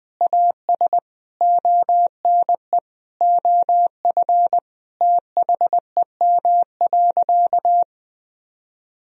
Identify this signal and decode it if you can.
Morse code, heard as a keyed tone, about 20 wpm, 700 Hz. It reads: AS ONE OF THEM.
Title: Sentences from Top 1000 Words - Review Part 2 - 20wpm